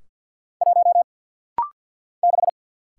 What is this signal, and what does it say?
Morse code, heard as a keyed tone, about 50 wpm, 700 Hz. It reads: 1 6
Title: Single Letter-Number - Rapid-Fire 50wpm